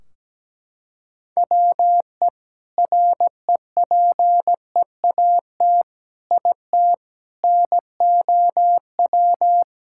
Morse code, heard as a keyed tone, about 17 wpm, 700 Hz. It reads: WE REPEAT IT NOW
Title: Sentences from Top 1600 Words 17wpm